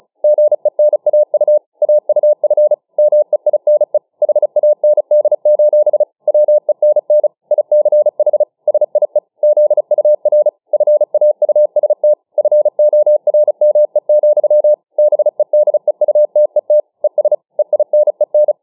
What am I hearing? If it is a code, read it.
Morse code, heard as a keyed tone, about 35 wpm, 600 Hz. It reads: GENAU AUF MEINE HAND: WENN ICH SIE ZUR FAUST FORME, BEDEUTET ES EINEN